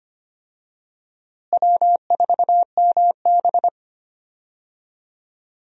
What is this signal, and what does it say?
Morse code, heard as a keyed tone, about 25 wpm, 700 Hz. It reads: W4MB
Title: USA Call Signs - 4 Characters - Repeated 2x - 25wpm